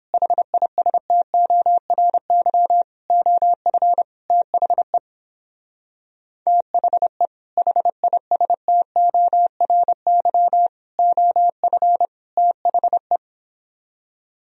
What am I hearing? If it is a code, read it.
Morse code, heard as a keyed tone, about 29 wpm, 700 Hz. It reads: HISTORY OF THE THE HISTORY OF THE